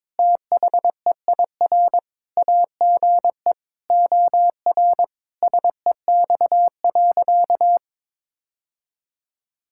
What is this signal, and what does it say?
Morse code, heard as a keyed tone, about 22 wpm, 700 Hz. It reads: THEIR AGE OR SEX.
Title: Sentences from Top 1600 Words 22wpm